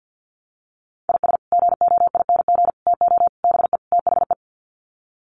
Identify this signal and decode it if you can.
Morse code, heard as a keyed tone, about 50 wpm, 700 Hz. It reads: IS GOING TO BE THE
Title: Sets of 5 Words 50wpm